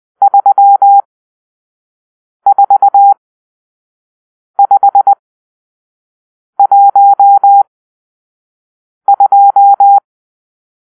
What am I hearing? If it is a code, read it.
Morse code, heard as a keyed tone, about 20 wpm, spaced as for 5 wpm, 800 Hz. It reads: 34512
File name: arrl / Grp-6 Drill-2